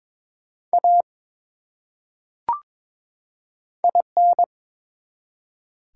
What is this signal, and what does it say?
Morse code, heard as a keyed tone, about 22 wpm, 700 Hz. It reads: A IN
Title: Top 180 QSO Elements 22wpm